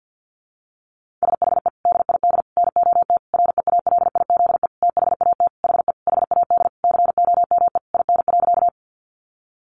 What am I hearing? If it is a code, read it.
Morse code, heard as a keyed tone, about 48 wpm, 700 Hz. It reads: SHE DID NOT REALIZE THAT HE HAD COME IN.
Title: Sentences from Top 1000 Words 50wpm